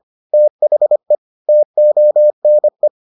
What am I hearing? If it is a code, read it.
Morse code, heard as a keyed tone, about 25 wpm, 600 Hz. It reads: THE TONE